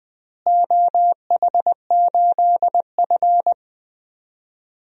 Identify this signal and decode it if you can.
Morse code, heard as a keyed tone, about 20 wpm, 700 Hz. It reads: OH8F
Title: DXCC-100 Call Signs - 4 Characters - Repeated 3x - 20wpm